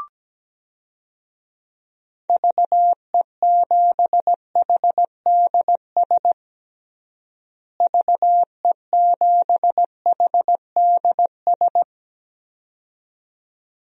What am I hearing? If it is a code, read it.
Morse code, heard as a keyed tone, about 17 wpm, 700 Hz. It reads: VE7HDS VE7HDS
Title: DXCC-100 Call Signs - 6 Characters - Repeated 2x - 17wpm